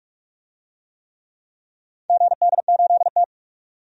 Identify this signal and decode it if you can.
Morse code, heard as a keyed tone, about 43 wpm, 700 Hz. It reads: GD8T
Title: DXCC-100 Call Signs - 4 Characters - 45wpm